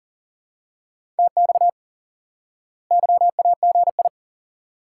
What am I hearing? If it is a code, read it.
Morse code, heard as a keyed tone, about 40 wpm, 700 Hz. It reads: TX YAGI
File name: IQE Course 3 - Lesson 89 - YAGI - Two Elements - 40wpm